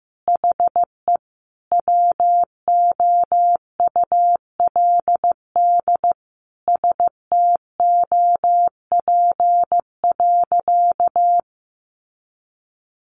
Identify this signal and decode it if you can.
Morse code, heard as a keyed tone, about 15 wpm, 700 Hz. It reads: HE WOULD STOP.
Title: Sentences from Top 1300 Words 15wpm